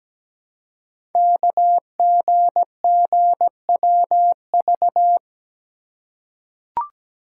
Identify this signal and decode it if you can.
Morse code, heard as a keyed tone, about 17 wpm, 700 Hz. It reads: KGGWV
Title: Code Groups - 5 Letters - 17wpm